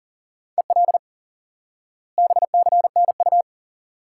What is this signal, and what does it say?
Morse code, heard as a keyed tone, about 40 wpm, 700 Hz. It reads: EL BCNU
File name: course / instant-qso-element-course-3 / IQE Course 3 - Lesson 89 - YAGI - Two Elements - 40wpm